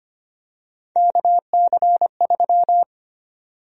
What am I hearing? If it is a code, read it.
Morse code, heard as a keyed tone, about 25 wpm, 700 Hz. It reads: KC3
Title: USA Call Sign Prefixes - 25wpm